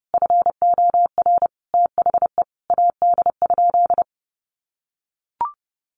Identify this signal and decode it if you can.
Morse code, heard as a keyed tone, about 30 wpm, 700 Hz. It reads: FOR THE AD?
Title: Sentences from Top 1300 Words 30wpm